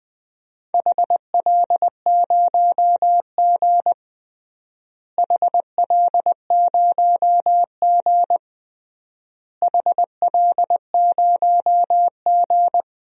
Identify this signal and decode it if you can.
Morse code, heard as a keyed tone, about 21 wpm, 700 Hz. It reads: HL0G HL0G HL0G
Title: DXCC-100 Call Signs - 4 Characters - Repeated 3x - 20wpm